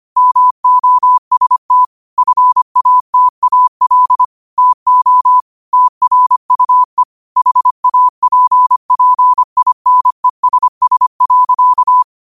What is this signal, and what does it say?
Morse code, heard as a keyed tone, about 25 wpm, 1000 Hz. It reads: MOST FATAL TO TRUE HAPPINESS.
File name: qotdfiles / 25wpm